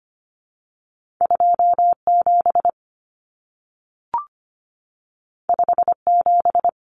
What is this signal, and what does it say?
Morse code, heard as a keyed tone, about 25 wpm, 700 Hz. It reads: 27 57
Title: Numbers - 2-Digit 25wpm